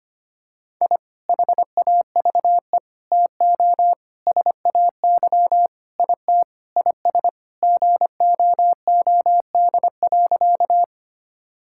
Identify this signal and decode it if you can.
Morse code, heard as a keyed tone, about 25 wpm, 700 Hz. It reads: I HAVE TO SAY IT IS GOOD.
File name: practice / 25wpm / Sentences with 4-Word Phrases - 25wpm